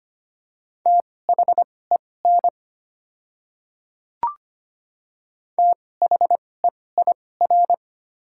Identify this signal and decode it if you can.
Morse code, heard as a keyed tone, about 25 wpm, spaced as for 18 wpm, 700 Hz. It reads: THEN THEIR
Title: 2x Character Spacing - Top 100 Words - 25wpm